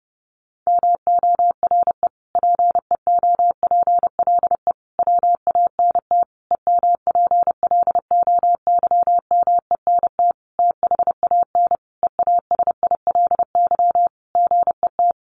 Morse code, heard as a keyed tone, about 30 wpm, 700 Hz. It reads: MORE PEOPLE WANT EMPLOYMENT THAN EASILY GET
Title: Sentences from Top 1500 Words 30wpm